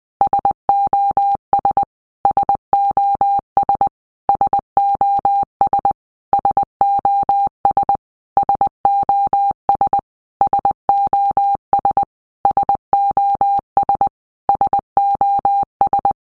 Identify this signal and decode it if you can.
Morse code, heard as a keyed tone, about 20 wpm, 800 Hz. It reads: SOS SOS SOS SOS SOS SOS SOS SOS
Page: SOS sound in Morse Code (Medium)